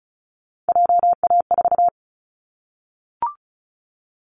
Morse code, heard as a keyed tone, about 34 wpm, 700 Hz. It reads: JA4
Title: Callsign Prefixes for Top 50 DXCC - 35wpm